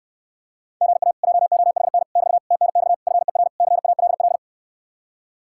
Morse code, heard as a keyed tone, about 69 wpm, 700 Hz. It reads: GA JOHN <BT> TNX FER CALL